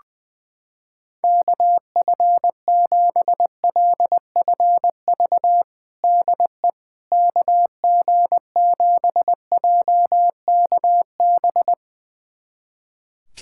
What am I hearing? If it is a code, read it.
Morse code, heard as a keyed tone, about 20 wpm, 700 Hz. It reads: KF7LFV DE KG7JKB
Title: USA Callsign DE Callsign 20wpm